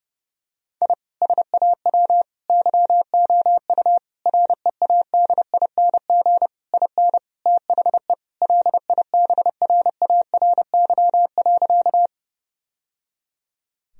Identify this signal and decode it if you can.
Morse code, heard as a keyed tone, about 30 wpm, 700 Hz. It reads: I SAW YOU READING IN THE LIBRARY.
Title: Sentences from Top 1000 Words - Review Part 4 - 30wpm